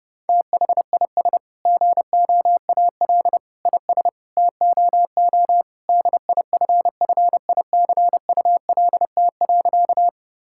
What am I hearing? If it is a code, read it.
Morse code, heard as a keyed tone, about 32 wpm, 700 Hz. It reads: THIS GOAL IS TOO DIFFICULT.
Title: Sentences from Top 1000 Words - Review Part 5 - 30wpm